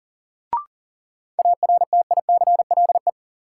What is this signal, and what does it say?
Morse code, heard as a keyed tone, about 40 wpm, 700 Hz. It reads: ARTICLE
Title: Sentences from Top 1000 Words - Review Part 5 - New Word Drill - 40wpm